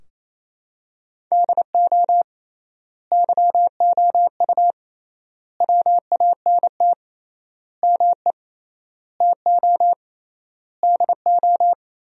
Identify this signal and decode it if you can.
Morse code, heard as a keyed tone, about 28 wpm, 700 Hz. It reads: DO YOU WANT ME TO DO
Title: 3x Word-Spacing - Sentences from Top 200 Words - 28wpm